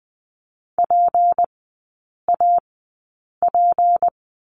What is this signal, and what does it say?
Morse code, heard as a keyed tone, about 20 wpm, spaced as for 8 wpm, 690 Hz. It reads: PAP
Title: 159 Lesson 26e - Letter g - Three Characters 20wpm